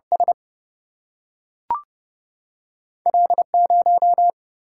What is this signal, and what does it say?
Morse code, heard as a keyed tone, about 30 wpm, 700 Hz. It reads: S L 0
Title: Single-Letter Single-Number 30wpm